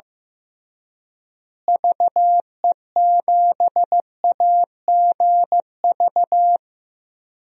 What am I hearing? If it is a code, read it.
Morse code, heard as a keyed tone, about 15 wpm, 700 Hz. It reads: VE7AGV
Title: DXCC-100 Call Signs - 6 Characters - Repeated 3x - 15wpm